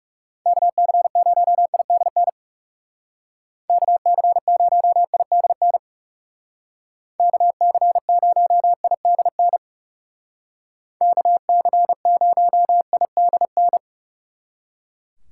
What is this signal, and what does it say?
Morse code, heard as a keyed tone, about 37 wpm, 700 Hz. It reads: KC0IDN KC0IDN KC0IDN KC0IDN
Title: Speed Racing - USA 2x3 Call Signs 30wpm